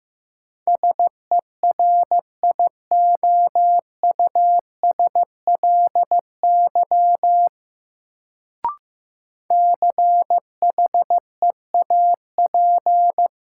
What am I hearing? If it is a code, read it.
Morse code, heard as a keyed tone, about 15 wpm, 700 Hz. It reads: SERIOUSLY CHEAP
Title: Sentences from Top 1200 Words - New Word Drill Alt - 15wpm